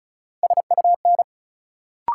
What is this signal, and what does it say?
Morse code, heard as a keyed tone, about 34 wpm, 700 Hz. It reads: SUN
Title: Sentences from Top 300 Words - New Word Drill - 35wpm